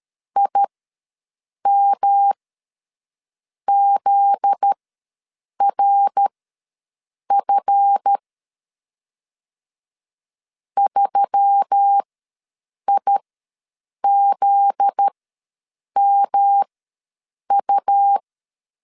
Morse code, heard as a keyed tone, about 13 wpm, spaced as for 7 wpm, 800 Hz. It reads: IMZRF 3IZMU
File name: LEZIONI / 7 GF9Z / 7 GF9Z ws 12wpm 3321